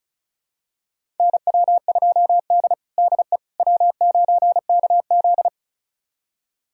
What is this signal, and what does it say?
Morse code, heard as a keyed tone, about 36 wpm, 700 Hz. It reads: NW2D DE W9KZ
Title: USA Callsign DE Callsign - 4 Characters - 35wpm